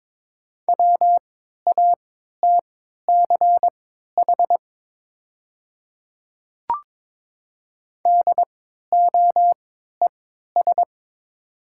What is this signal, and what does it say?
Morse code, heard as a keyed tone, about 22 wpm, spaced as for 12 wpm, 700 Hz. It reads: WATCH DOES
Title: 3x Character Spacing - Top 500 Words - 22wpm